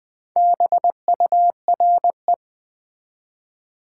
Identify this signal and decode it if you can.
Morse code, heard as a keyed tone, about 20 wpm, 700 Hz. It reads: BURE
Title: Build A Word - 4 Letters At A Time - 11-Letter Words - 20wpm